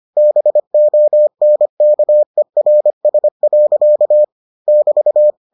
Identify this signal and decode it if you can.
Morse code, heard as a keyed tone, about 25 wpm, 600 Hz. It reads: BONKERS. <BT>